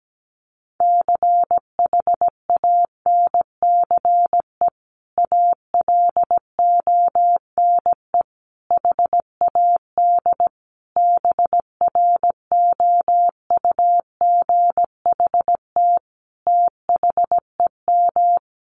Morse code, heard as a keyed tone, about 17 wpm, 700 Hz. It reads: CHANCE ALONE HAD BROUGHT THEM
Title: Sentences from Top 1000 Words 17wpm